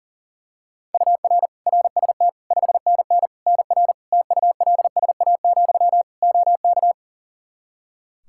Morse code, heard as a keyed tone, about 40 wpm, 700 Hz. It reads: UR RST 5NN NR TULSA, OK
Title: Practice QSO - Exchange 1 - Beginning - 40wpm